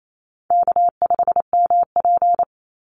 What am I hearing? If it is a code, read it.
Morse code, heard as a keyed tone, about 28 wpm, 700 Hz. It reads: K5MP
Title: Call Signs Repeated 2x - 28wpm